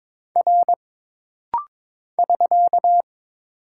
Morse code, heard as a keyed tone, about 22 wpm, 700 Hz. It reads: R <SK>
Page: This rapid-fire timing is geared to push your ICR (Instant Character Recognition) skills.